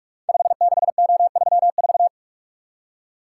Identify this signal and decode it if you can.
Morse code, heard as a keyed tone, about 46 wpm, 700 Hz. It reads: 56O34
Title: Code Groups - 5 Letter:Numbers - 45wpm